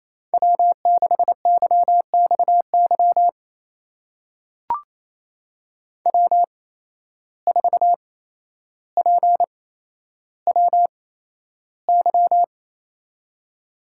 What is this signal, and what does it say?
Morse code, heard as a keyed tone, about 28 wpm, spaced as for 12 wpm, 700 Hz. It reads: W6YXY W4PWY
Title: USA Call Signs - 5 Characters - 8x Character Spacing - 28wpm